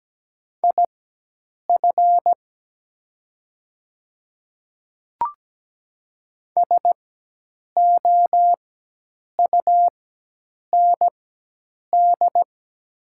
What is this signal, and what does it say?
Morse code, heard as a keyed tone, about 17 wpm, spaced as for 8 wpm, 700 Hz. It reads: IF SOUND
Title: 4x Character Spacing - Top 100 Words - 17wpm